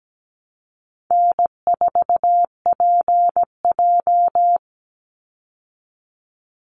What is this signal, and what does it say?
Morse code, heard as a keyed tone, about 17 wpm, 700 Hz. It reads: N4PJ
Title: Easy Call Signs Repeated 2x - 17wpm